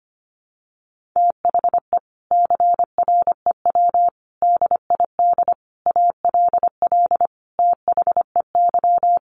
Morse code, heard as a keyed tone, about 25 wpm, 700 Hz. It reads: THE CREW DID ALL THEY
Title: Sentences from Top 1500 Words 25wpm